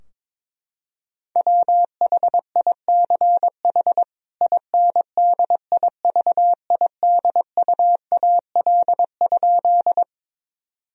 Morse code, heard as a keyed tone, about 22 wpm, 700 Hz. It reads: WHICH INDIVIDUAL?
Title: Sentences from Top 1000 Words - Review Part 4 - 22wpm